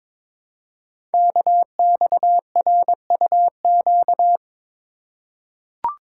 Morse code, heard as a keyed tone, about 23 wpm, 700 Hz. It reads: KXRUQ
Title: Code Groups - 5 Letters - 22wpm